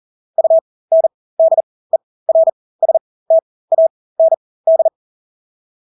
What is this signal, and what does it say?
Morse code, heard as a keyed tone, about 40 wpm, spaced as for 20 wpm, 650 Hz. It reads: UNDERSTAND
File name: Engelske ord 40-20 wpm